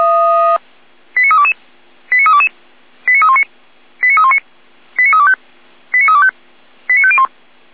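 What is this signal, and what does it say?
Five-tone selective call (ZVEI), heard as quick runs of five tones, tones of about 70 ms. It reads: 89322 89322 89319 89319 89325 89325 89501
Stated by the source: ZVEI five-tone selective call 89068 received on an SDR.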